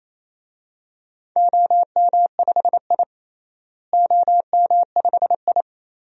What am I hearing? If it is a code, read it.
Morse code, heard as a keyed tone, about 28 wpm, 700 Hz. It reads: OM5I OM5I
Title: DXCC-100 Call Signs - 4 Characters - Repeated 3x - 28wpm